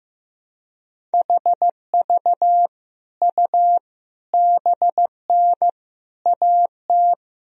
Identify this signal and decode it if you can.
Morse code, heard as a keyed tone, about 15 wpm, 700 Hz. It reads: HV U BN AT
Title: Sentences from Top 300 Words and Abbreviations 15wpm